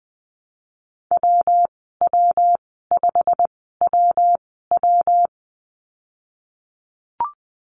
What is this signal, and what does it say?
Morse code, heard as a keyed tone, about 20 wpm, spaced as for 15 wpm, 700 Hz. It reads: WW5WW
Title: 128 Lesson 21g - Number 6 - Call Signs 20wpm